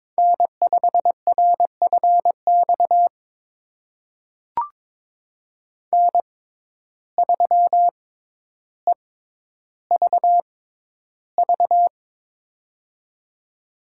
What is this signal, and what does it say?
Morse code, heard as a keyed tone, about 22 wpm, spaced as for 11 wpm, 700 Hz. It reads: N5RFX N3EVV
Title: USA Call Signs - 5 Characters - 6x Character Spacing - 22wpm